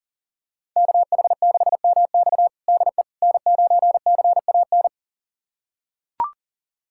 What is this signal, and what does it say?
Morse code, heard as a keyed tone, about 37 wpm, 700 Hz. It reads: KH6MX DE N9CAN
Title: USA Callsign DE Callsign - 5 Characters - 40wpm